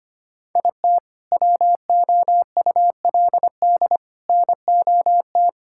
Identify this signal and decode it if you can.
Morse code, heard as a keyed tone, about 25 wpm, 700 Hz. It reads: IT WOULD NOT
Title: Sentences from Top 1000 Words - Review Part 5 - 25wpm